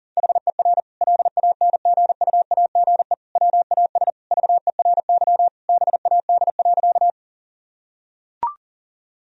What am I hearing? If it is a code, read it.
Morse code, heard as a keyed tone, about 40 wpm, 700 Hz. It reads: HER LANGUAGE WAS VERY BAD.
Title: Sentences from Top 1000 Words - Review Part 3 - 40wpm